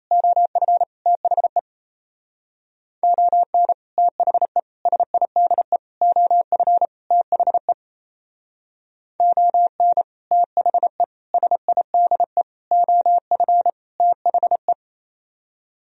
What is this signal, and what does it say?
Morse code, heard as a keyed tone, about 34 wpm, 700 Hz. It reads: OF THE ON THE SIDE OF THE ON THE SIDE OF THE
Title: Speed-Racing - Sets of 5 Words 28wpm